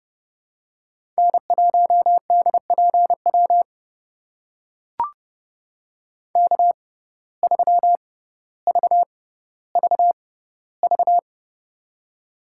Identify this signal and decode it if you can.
Morse code, heard as a keyed tone, about 30 wpm, spaced as for 15 wpm, 700 Hz. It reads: N1DPW K3VVV
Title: USA Call Signs - 5 Characters - 6x Character Spacing - 30wpm